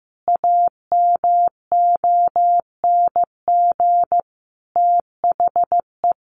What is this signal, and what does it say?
Morse code, heard as a keyed tone, about 14 wpm, 700 Hz. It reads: AMONG THE
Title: Sentences from Top 1400 Words 15wpm